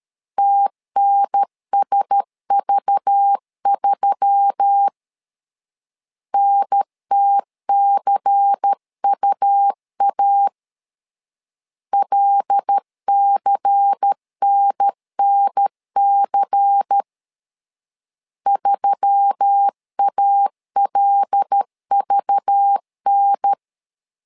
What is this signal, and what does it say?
Morse code, heard as a keyed tone, about 13 wpm, 800 Hz. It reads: TNSV3 NTCUA LCNNC 3ALVN
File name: LEZIONI / 5 LBVC / 5 LBVC 12wpm 1955